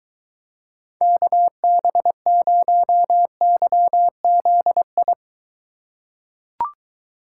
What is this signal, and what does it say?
Morse code, heard as a keyed tone, about 20 wpm, 700 Hz. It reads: KB0YZI